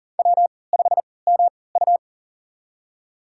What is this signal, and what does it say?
Morse code, heard as a keyed tone, about 40 wpm, spaced as for 23 wpm, 700 Hz. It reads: W5MU
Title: USA Call Signs - 3x Character Spacing - 40wpm